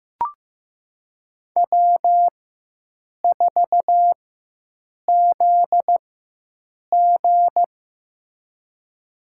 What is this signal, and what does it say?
Morse code, heard as a keyed tone, about 15 wpm, spaced as for 7 wpm, 700 Hz. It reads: W4ZG
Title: USA Call Signs - 4 Characters - 4x Character Spacing - 15wpm